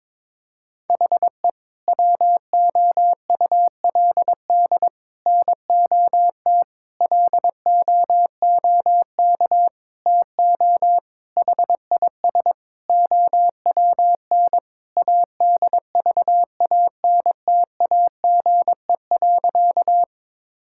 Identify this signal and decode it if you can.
Morse code, heard as a keyed tone, about 22 wpm, 700 Hz. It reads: HE WOULD NOT LOOK TO HIS OWN ADVANTAGE.
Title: Sentences from Top 1000 Words - Review Part 6 - 22wpm